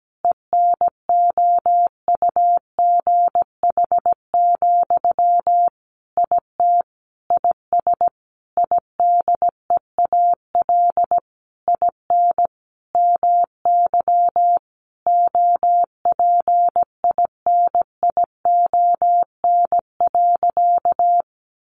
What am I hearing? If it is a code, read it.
Morse code, heard as a keyed tone, about 17 wpm, 700 Hz. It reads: ENOUGH, IT IS IDEAL IN MY OPINION.
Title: Sentences from Top 1400 Words 17wpm